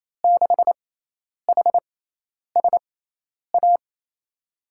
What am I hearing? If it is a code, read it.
Morse code, heard as a keyed tone, about 28 wpm, spaced as for 10 wpm, 700 Hz. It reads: 6HSA
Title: USA Call Signs - 5 Characters - 6x Character Spacing - 28wpm